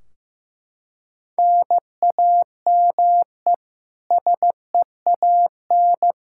Morse code, heard as a keyed tone, about 15 wpm, 700 Hz. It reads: NAME SEAN
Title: Practice QSO - Exchange 1 - Beginning - 15wpm